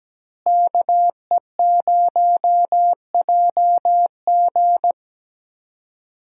Morse code, heard as a keyed tone, about 17 wpm, 700 Hz. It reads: KE0JG
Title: USA Call Signs - 5 Characters - Repeated 3x - 17wpm